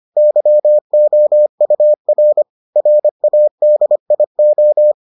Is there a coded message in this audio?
Yes: Morse code from 0.2 to 4.9 s, heard as a keyed tone, about 25 wpm, 600 Hz. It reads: YOUR RADIO